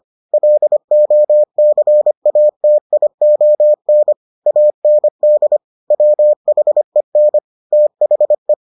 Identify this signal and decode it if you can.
Morse code, heard as a keyed tone, about 25 wpm, 600 Hz. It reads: LOCATION AND WHEN THE